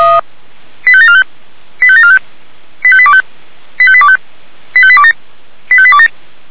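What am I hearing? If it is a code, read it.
Five-tone selective call (ZVEI), heard as quick runs of five tones, tones of about 70 ms. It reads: 85745 85745 85825 85825 85827 85827
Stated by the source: ZVEI five-tone selective call received on SDR.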